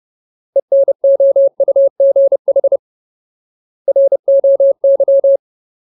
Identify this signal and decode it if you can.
Morse code, heard as a keyed tone, about 30 wpm, 550 Hz. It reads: ENOUGH ROY
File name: Ultimate Word List / 30WPM Ultimate